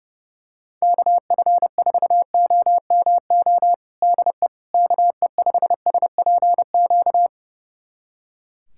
Morse code, heard as a keyed tone, about 32 wpm, 700 Hz. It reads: KF4OMO DE KE5SPQ
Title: USA Callsign DE Callsign - 6 Characters - 30wpm